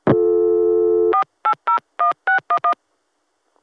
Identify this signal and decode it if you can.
DTMF, heard as telephone touch tones, tones of about 100 ms. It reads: 4801611